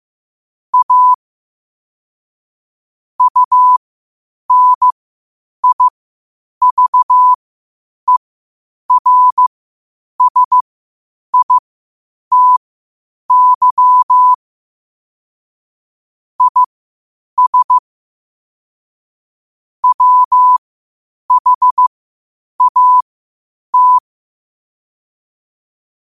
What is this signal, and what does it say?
Morse code, heard as a keyed tone, about 15 wpm, spaced as for 8 wpm, 1000 Hz. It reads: A UNIVERSITY IS WHAT
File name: qotdfiles / 8wpm